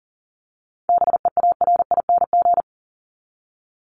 Morse code, heard as a keyed tone, about 40 wpm, 700 Hz. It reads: BEARING
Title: ING Suffix - Top 100 Words with Base Words 40wpm